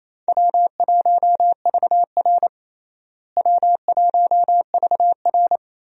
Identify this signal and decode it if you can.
Morse code, heard as a keyed tone, about 28 wpm, 700 Hz. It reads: W1VR W1VR
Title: USA Call Signs - 4 Characters - Repeated 2x - 28wpm